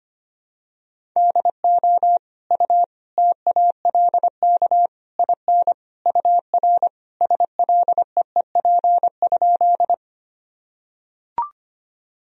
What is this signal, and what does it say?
Morse code, heard as a keyed tone, about 25 wpm, 700 Hz. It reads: DO U TALK IN UR SLEEP?
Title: Sentences from Top 500 Words and Abbreviations 25wpm